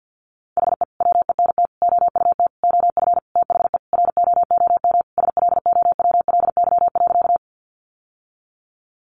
Morse code, heard as a keyed tone, about 50 wpm, 700 Hz. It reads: HE WENT OUT OF THE ROOM SLOWLY.
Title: Sentences from Top 1100 Words 50wpm